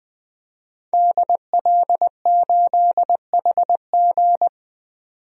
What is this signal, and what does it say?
Morse code, heard as a keyed tone, about 20 wpm, 700 Hz. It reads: DL8HG
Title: DXCC-100 Call Signs - 5 Characters - 20wpm